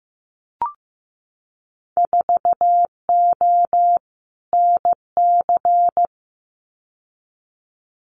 Morse code, heard as a keyed tone, about 15 wpm, 700 Hz. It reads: 4O NC
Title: Winter Field Day Exchanges 15wpm